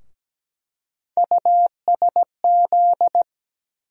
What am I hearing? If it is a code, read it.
Morse code, heard as a keyed tone, about 17 wpm, 700 Hz. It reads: USZ
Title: USA Call Sign Postfixes - 3 Letters 17wpm